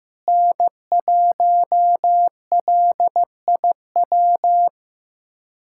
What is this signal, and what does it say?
Morse code, heard as a keyed tone, about 15 wpm, 700 Hz. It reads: N1LIW
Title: USA Callsign DE Callsign - 5 Characters - 15wpm